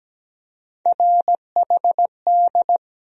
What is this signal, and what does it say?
Morse code, heard as a keyed tone, about 17 wpm, 700 Hz. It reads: RHD
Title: USA Call Sign Postfixes - 3 Letters 17wpm